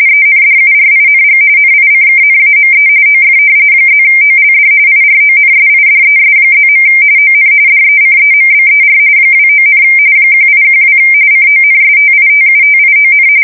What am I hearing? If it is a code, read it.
RTTY, about 45.45 baud. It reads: ENDURING NICKNAME OF +OLD IRONSIDES+. ON DECEMBER 29, 1812 CONSTITUTION
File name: fsk170 / usnavy2